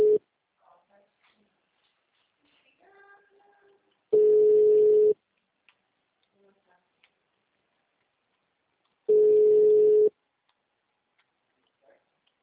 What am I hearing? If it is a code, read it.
Ringback tone (Europe, 425 Hz), with speech.